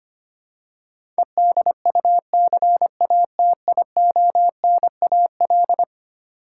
Morse code, heard as a keyed tone, about 25 wpm, 700 Hz. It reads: EDUCATIONAL
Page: morse-code-ninja-practice-25wpm